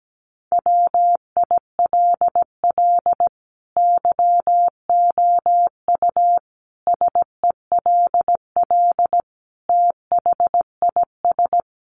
Morse code, heard as a keyed tone, about 17 wpm, 700 Hz. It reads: WILL YOU SELL THIS
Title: Sentences from Top 1200 Words 17wpm